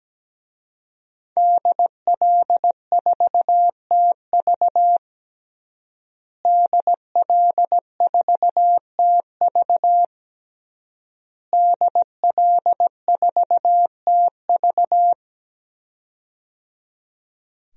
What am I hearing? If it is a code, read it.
Morse code, heard as a keyed tone, about 17 wpm, 700 Hz. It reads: DL4TV DL4TV DL4TV
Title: DXCC-100 Call Signs - 5 Characters - Repeated 3x - 17wpm